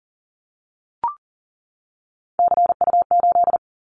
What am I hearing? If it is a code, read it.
Morse code, heard as a keyed tone, about 40 wpm, 700 Hz. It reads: CU8
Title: Callsign Prefixes for Top 50 DXCC - 40wpm